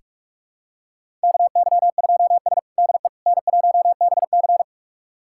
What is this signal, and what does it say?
Morse code, heard as a keyed tone, about 42 wpm, 700 Hz. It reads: KY2S DE N1BC
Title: USA Callsign DE Callsign - 4 Characters - 45wpm